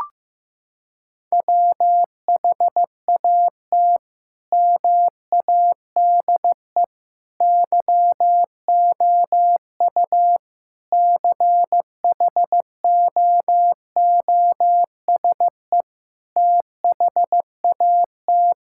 Morse code, heard as a keyed tone, about 15 wpm, 690 Hz. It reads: WHAT MADE YOU CHOOSE THAT
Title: Sentences from Top 1000 Words - Review Part 5 - 15wpm